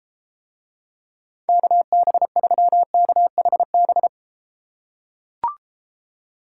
Morse code, heard as a keyed tone, about 30 wpm, 700 Hz. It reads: KB3KHB